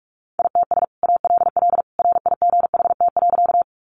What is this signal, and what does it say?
Morse code, heard as a keyed tone, about 44 wpm, 700 Hz. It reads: ITS ALL RIGHT.
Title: Sentences from Top 600 Words 45wpm